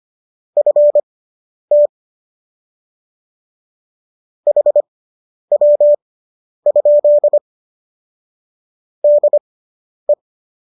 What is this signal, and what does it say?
Morse code, heard as a keyed tone, about 25 wpm, spaced as for 10 wpm, 590 Hz. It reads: FT HW? DE